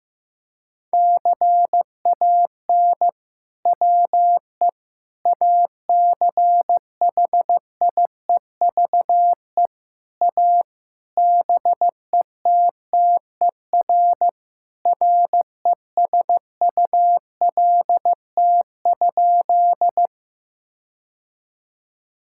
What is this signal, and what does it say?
Morse code, heard as a keyed tone, about 15 wpm, 700 Hz. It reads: CAN WE ACHIEVE A BETTER RESULT?
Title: Sentences from Top 1000 Words - Review Part 6 - 15wpm